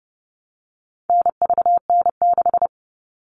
Morse code, heard as a keyed tone, about 30 wpm, 700 Hz. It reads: NVN6
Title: All Cut Numbers - 4-Digit 30wpm